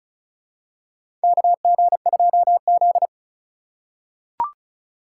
Morse code, heard as a keyed tone, about 36 wpm, 700 Hz. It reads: KG2Z
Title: USA Call Signs - 4 Characters - 3x Character Spacing - 35wpm